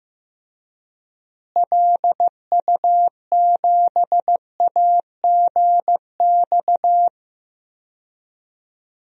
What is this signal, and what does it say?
Morse code, heard as a keyed tone, about 14 wpm, 700 Hz. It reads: LU7AGX
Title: DXCC-100 Call Signs - 6 Characters - Repeated 3x - 15wpm